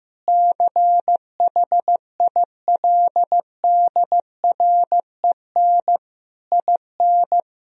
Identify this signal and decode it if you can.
Morse code, heard as a keyed tone, about 15 wpm, 700 Hz. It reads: CHILDREN IN
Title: Sentences from Top 1000 Words - Review Part 2 - 15wpm